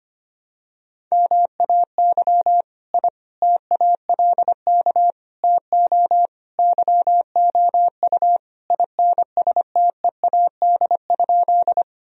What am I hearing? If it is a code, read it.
Morse code, heard as a keyed tone, about 25 wpm, 700 Hz. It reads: MAY I TALK TO YOU INSTEAD?
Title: Sentences from Top 1000 Words - Review Part 4 - 25wpm